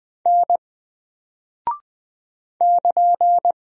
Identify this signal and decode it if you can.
Morse code, heard as a keyed tone, about 20 wpm, 700 Hz. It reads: N <KN>
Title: Single Character (Letter:Number:Punctuation:Prosign) 20wpm